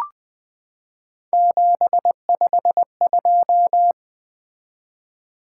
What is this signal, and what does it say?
Morse code, heard as a keyed tone, about 20 wpm, 700 Hz. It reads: 752
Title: Three Letter-Number Combinations - 20wpm